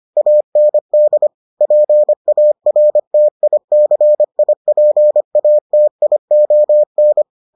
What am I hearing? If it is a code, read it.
Morse code, heard as a keyed tone, about 25 wpm, 600 Hz. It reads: AND PARTICIPATION